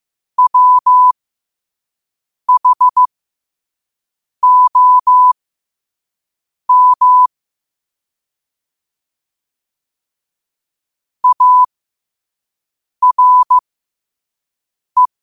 Morse code, heard as a keyed tone, about 15 wpm, spaced as for 5 wpm, 1000 Hz. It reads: WHOM ARE
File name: qotdfiles / 5wpm